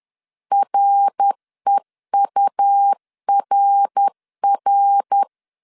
Morse code, heard as a keyed tone, about 10 wpm, 800 Hz. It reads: REURR
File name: CW / DET / 1 3 1500C S B 10wpm 3335